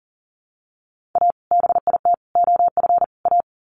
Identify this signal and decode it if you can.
Morse code, heard as a keyed tone, about 40 wpm, 700 Hz. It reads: A BIT OF A
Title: Sets of 4 Words 40wpm